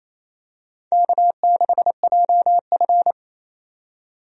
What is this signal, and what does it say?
Morse code, heard as a keyed tone, about 28 wpm, 700 Hz. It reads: K6JF
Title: USA Call Signs - 4 Characters - 28wpm